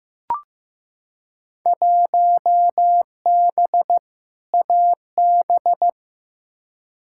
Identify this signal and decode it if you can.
Morse code, heard as a keyed tone, about 15 wpm, 700 Hz. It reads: 1B AB
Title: Field Day Exchanges - 15wpm